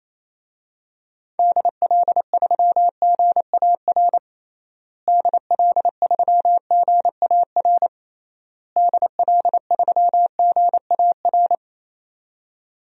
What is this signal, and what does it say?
Morse code, heard as a keyed tone, about 27 wpm, 700 Hz. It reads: DL3GAR DL3GAR DL3GAR
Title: DXCC-100 Call Signs - 6 Characters - Repeated 3x - 28wpm